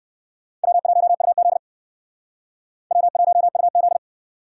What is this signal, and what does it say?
Morse code, heard as a keyed tone, about 61 wpm, 700 Hz. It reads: W1UZ W1UZ